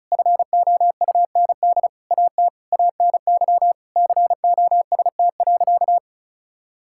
Morse code, heard as a keyed tone, about 36 wpm, 700 Hz. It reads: FOUND AT ANY COST.
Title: Sentences from Top 1000 Words - Review Part 5 - 35wpm